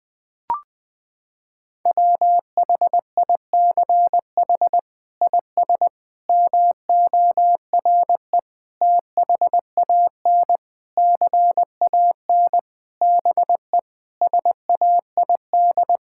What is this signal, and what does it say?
Morse code, heard as a keyed tone, about 20 wpm, 700 Hz. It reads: WHICH IS MORE THAN CAN BE SAID
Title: Sets of 8 Words - 20wpm